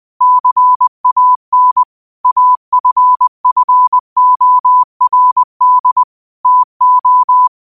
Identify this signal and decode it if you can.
Morse code, heard as a keyed tone, about 20 wpm, 1000 Hz. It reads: CAN AFFORD TO
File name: 20wpm